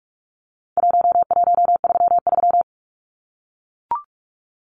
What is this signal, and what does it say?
Morse code, heard as a keyed tone, about 44 wpm, 700 Hz. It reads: 1133
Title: Numbers - 4-Digit 45wpm